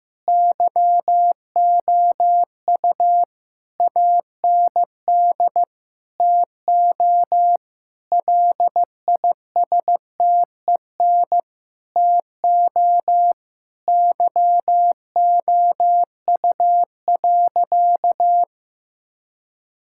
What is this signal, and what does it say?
Morse code, heard as a keyed tone, about 15 wpm, 700 Hz. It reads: YOU AND TO LISTEN TO YOU.
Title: Sentences from Top 1600 Words 15wpm